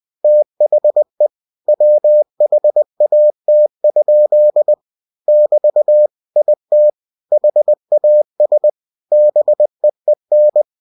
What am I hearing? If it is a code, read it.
Morse code, heard as a keyed tone, about 20 wpm, 600 Hz. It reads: THE WHAT? <BT> IT HAS BEEN
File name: cw-books / winnie-the-pooh / winnie-20wpm-page-0096